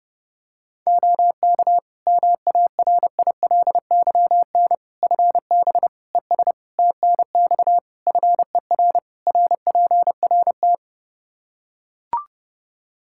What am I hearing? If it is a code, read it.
Morse code, heard as a keyed tone, about 32 wpm, 700 Hz. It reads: OK MARILYN FB ES TNX FER RPRT
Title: Practice QSO - Exchange 2 - Info - 30wpm